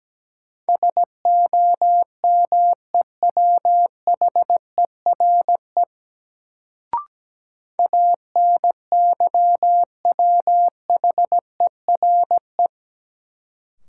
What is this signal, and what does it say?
Morse code, heard as a keyed tone, about 17 wpm, 700 Hz. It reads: SOMEWHERE ANYWHERE
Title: Sentences from Top 1400 Words - New Word Drill Alt - 17wpm